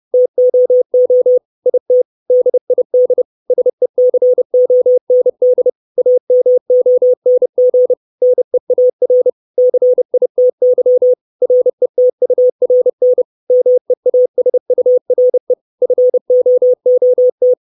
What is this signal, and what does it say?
Morse code, heard as a keyed tone, about 30 wpm, 500 Hz. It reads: TOO IT DID SECOND AMONG NEAR CITY RETURN MEASURE FOOT
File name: 632 Words 30WPM